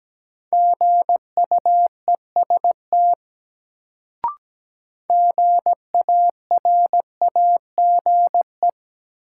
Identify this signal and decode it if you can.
Morse code, heard as a keyed tone, about 17 wpm, 700 Hz. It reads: GUEST GARAGE
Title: Sentences from Top 1600 Words - New Word Drill Alt - 17wpm